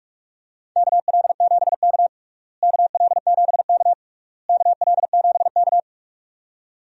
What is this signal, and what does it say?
Morse code, heard as a keyed tone, about 41 wpm, 700 Hz. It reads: KL7K KL7K KL7K
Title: DXCC-100 Call Signs - 4 Characters - Repeated 3x - 45wpm